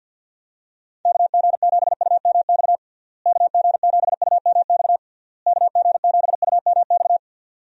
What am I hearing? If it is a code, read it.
Morse code, heard as a keyed tone, about 51 wpm, 700 Hz. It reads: KG7UMX KG7UMX KG7UMX
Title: USA Call Signs - 6 Characters - Repeated 3x - 50wpm